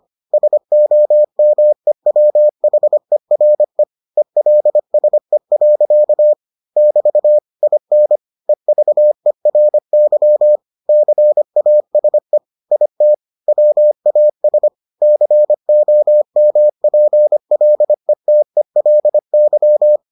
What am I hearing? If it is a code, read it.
Morse code, heard as a keyed tone, about 25 wpm, 600 Hz. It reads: SOMEWHERE ELSE. <BT> IN EVERY CASE IT WAS COMPLETELY